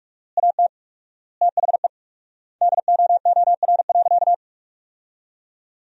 Morse code, heard as a keyed tone, about 41 wpm, 700 Hz. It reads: AT THE DOOR.
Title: 4x Word-Spacing - Sentences from Top 400 Words - 45wpm